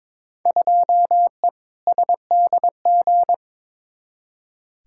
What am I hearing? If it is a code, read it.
Morse code, heard as a keyed tone, about 22 wpm, 700 Hz. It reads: 2E SDG
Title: Field Day Exchanges - 22wpm